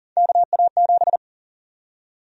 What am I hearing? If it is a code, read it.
Morse code, heard as a keyed tone, about 38 wpm, 700 Hz. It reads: KA7
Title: USA Call Sign Prefixes - Repeated 2x - 40wpm